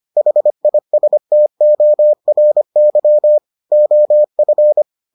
Morse code, heard as a keyed tone, about 25 wpm, 600 Hz. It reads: HISTORY OF